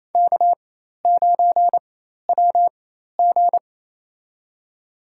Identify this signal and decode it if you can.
Morse code, heard as a keyed tone, about 28 wpm, spaced as for 13 wpm, 700 Hz. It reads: K9WG
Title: USA Call Signs - 4 Characters - 4x Character Spacing - 28wpm